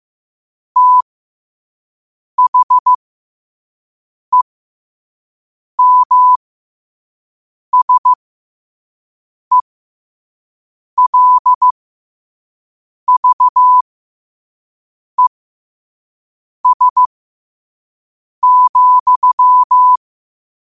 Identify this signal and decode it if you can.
Morse code, heard as a keyed tone, about 15 wpm, spaced as for 5 wpm, 1000 Hz. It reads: THEMSELVES,
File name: qotdfiles / 5wpm